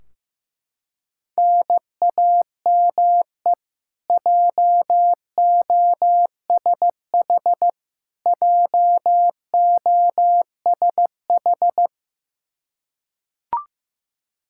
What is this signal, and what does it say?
Morse code, heard as a keyed tone, about 16 wpm, 700 Hz. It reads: NAME JOSH JOSH
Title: K1USN SST QSO Practice 15wpm